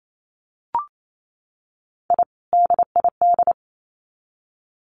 Morse code, heard as a keyed tone, about 28 wpm, 700 Hz. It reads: I DID
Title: Sets of 2 Words 28wpm